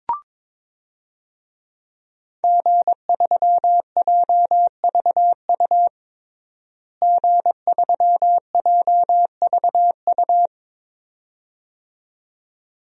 Morse code, heard as a keyed tone, about 22 wpm, 700 Hz. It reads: G3JVU G3JVU
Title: DXCC-100 Call Signs - 5 Characters - Repeated 2x - 22wpm